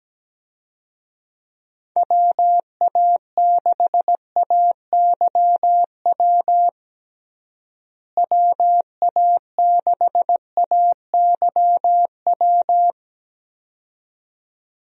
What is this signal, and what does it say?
Morse code, heard as a keyed tone, about 17 wpm, 700 Hz. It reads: WA6AYW WA6AYW
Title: USA Call Signs - 6 Characters - Repeated 2x - 17wpm